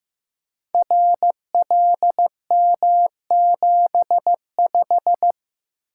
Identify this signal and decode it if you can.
Morse code, heard as a keyed tone, about 15 wpm, 700 Hz. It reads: RLM75
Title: Code Groups - 5 Letter:Numbers - 15wpm